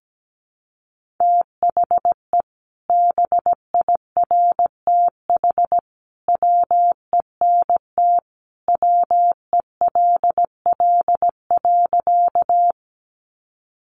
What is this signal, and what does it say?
Morse code, heard as a keyed tone, about 17 wpm, 700 Hz. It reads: THE BIRTH WENT WELL.
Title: Sentences from Top 800 Words 17wpm